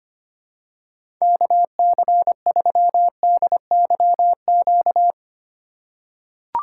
Morse code, heard as a keyed tone, about 25 wpm, 700 Hz. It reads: KC3DYQ
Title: USA Call Signs - 6 Characters - 2x Character Spacing - 25wpm